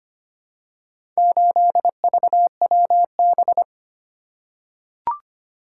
Morse code, heard as a keyed tone, about 25 wpm, 700 Hz. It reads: 8VWB
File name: practice / 25wpm / Four Letter-Number Combinations - 25wpm